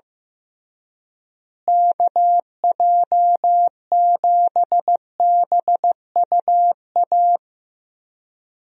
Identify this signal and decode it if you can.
Morse code, heard as a keyed tone, about 15 wpm, 700 Hz. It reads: KJ7BUA
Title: USA Call Signs - 6 Characters - Repeated 3x - 15wpm